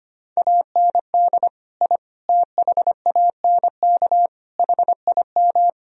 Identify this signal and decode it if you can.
Morse code, heard as a keyed tone, about 25 wpm, 700 Hz. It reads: AND I THANK HIM
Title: Sentences from Top 1000 Words - Review Part 3 - 25wpm